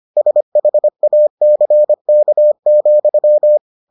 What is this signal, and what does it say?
Morse code, heard as a keyed tone, about 25 wpm, 600 Hz. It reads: SHACK,